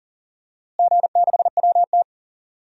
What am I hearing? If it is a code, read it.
Morse code, heard as a keyed tone, about 40 wpm, 700 Hz. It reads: G6WT
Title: DXCC-100 Call Signs - 4 Characters - Repeated 2x - 40wpm